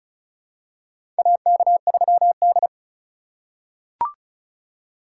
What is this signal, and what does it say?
Morse code, heard as a keyed tone, about 34 wpm, 700 Hz. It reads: AK3D
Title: USA Call Signs - 4 Characters - 2x Character Spacing - 35wpm